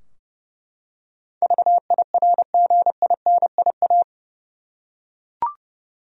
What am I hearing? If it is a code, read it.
Morse code, heard as a keyed tone, about 31 wpm, 700 Hz. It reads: VIRGINIA
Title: USA Common Names - 30wpm